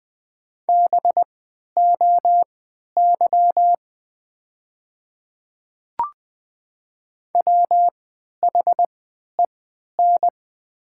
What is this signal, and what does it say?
Morse code, heard as a keyed tone, about 20 wpm, spaced as for 11 wpm, 700 Hz. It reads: BOY WHEN
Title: 3x Character Spacing - Top 300 Words - 20wpm